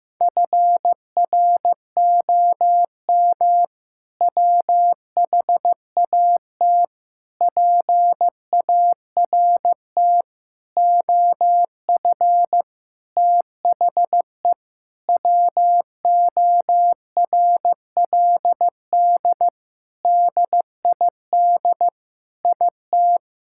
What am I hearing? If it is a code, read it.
Morse code, heard as a keyed tone, about 16 wpm, 700 Hz. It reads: FROM WHAT PART OF THE WORLD DID IT
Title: Sentences from Top 1000 Words - Review Part 6 - 15wpm